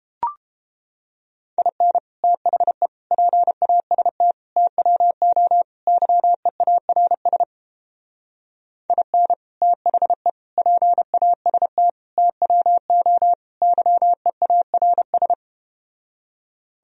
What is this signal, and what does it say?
Morse code, heard as a keyed tone, about 30 wpm, 700 Hz. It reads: IN THE PAST TWO YEARS IN THE PAST TWO YEARS